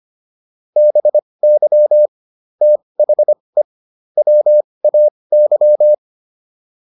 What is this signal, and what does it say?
Morse code, heard as a keyed tone, about 25 wpm, spaced as for 20 wpm, 600 Hz. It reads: BY THE WAY